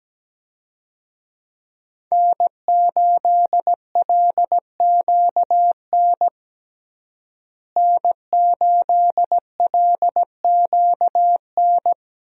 Morse code, heard as a keyed tone, about 17 wpm, 700 Hz. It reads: N8LQN N8LQN
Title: USA Call Signs - 5 Characters - Repeated 2x - 17wpm